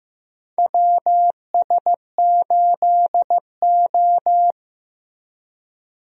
Morse code, heard as a keyed tone, about 15 wpm, 700 Hz. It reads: WS8O
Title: USA Call Signs - 4 Characters - Repeated 2x - 15wpm